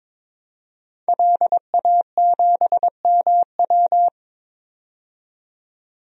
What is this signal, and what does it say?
Morse code, heard as a keyed tone, about 22 wpm, 700 Hz. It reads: LA7MW
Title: DXCC-100 Call Signs - 5 Characters - 22wpm